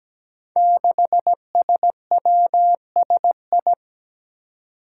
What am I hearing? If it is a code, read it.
Morse code, heard as a keyed tone, about 17 wpm, 700 Hz. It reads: 6SWSI
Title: Five Challenging Letter-Number Combinations 17wpm